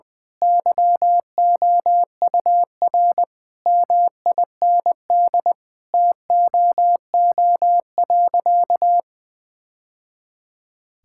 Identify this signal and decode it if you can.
Morse code, heard as a keyed tone, about 20 wpm, 700 Hz. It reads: YOUR MIND TOO.
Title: Sentences from Top 1000 Words - Review Part 6 - 20wpm